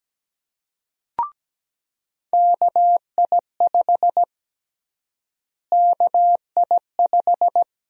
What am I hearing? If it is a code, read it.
Morse code, heard as a keyed tone, about 17 wpm, 700 Hz. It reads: KI5 KI5
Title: USA Call Sign Prefixes - Repeated 2x - 17wpm